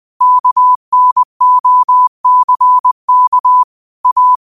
Morse code, heard as a keyed tone, about 20 wpm, 1000 Hz. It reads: KNOCK A
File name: qotdfiles / 20wpm